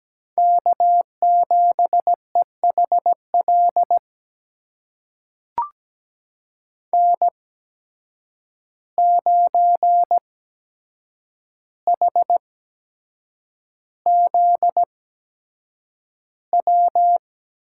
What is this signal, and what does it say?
Morse code, heard as a keyed tone, about 17 wpm, spaced as for 7 wpm, 700 Hz. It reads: K7EHL N9HZW
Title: USA Call Signs - 5 Characters - 8x Character Spacing - 17wpm